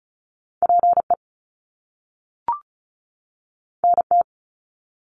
Morse code, heard as a keyed tone, about 36 wpm, 700 Hz. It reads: PE NT
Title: Canadian Provinces and Territories 35wpm